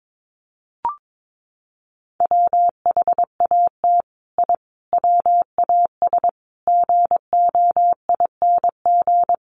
Morse code, heard as a keyed tone, about 22 wpm, 700 Hz. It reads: WHAT I WAS GOING
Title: Sets of 5 Words Encore2 22wpm